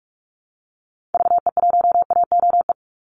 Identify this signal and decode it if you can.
Morse code, heard as a keyed tone, about 44 wpm, 700 Hz. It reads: VE1AOE
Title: Call Signs - 45wpm